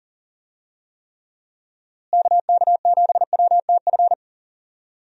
Morse code, heard as a keyed tone, about 39 wpm, 700 Hz. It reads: KK7WTF
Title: USA Call Signs - 6 Characters - Repeated 2x - 40wpm